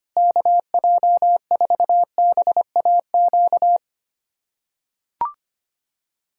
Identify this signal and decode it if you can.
Morse code, heard as a keyed tone, about 25 wpm, 700 Hz. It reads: KJ4BAQ
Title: USA Callsign DE Callsign - 6 Characters - 25wpm